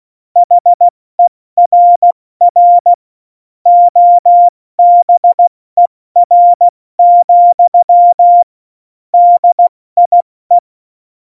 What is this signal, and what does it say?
Morse code, heard as a keyed tone, about 16 wpm, 700 Hz. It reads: HERR OBER, DIE
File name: Morsekurs / Witze / Witz 22. Oktober 2019, 70 F80